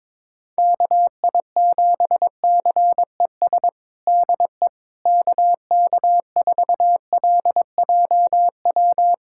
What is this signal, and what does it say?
Morse code, heard as a keyed tone, about 22 wpm, 700 Hz. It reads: KI7CES DE KK4LJW
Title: USA Callsign DE Callsign 22wpm